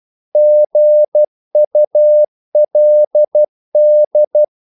Morse code, heard as a keyed tone, about 12 wpm, 600 Hz. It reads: GULD
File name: Korte ord DK 12wpm